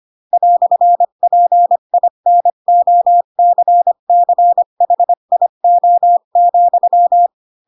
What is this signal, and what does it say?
Morse code, heard as a keyed tone, about 25 wpm, 700 Hz. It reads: "PINOCCHIO,
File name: Pinocchio / Pinocchio ch3 20 25 ik1hge